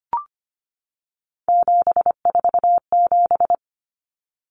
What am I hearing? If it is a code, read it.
Morse code, heard as a keyed tone, about 25 wpm, 700 Hz. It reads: 747
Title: Numbers - 3-Digit 25wpm